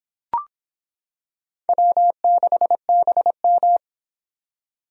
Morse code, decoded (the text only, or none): W6BM